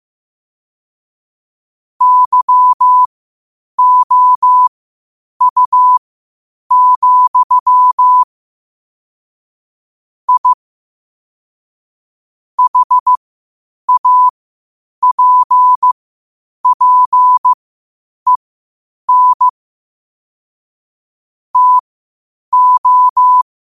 Morse code, heard as a keyed tone, about 15 wpm, spaced as for 8 wpm, 1000 Hz. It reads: YOU, I HAPPEN TO
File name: qotdfiles / 8wpm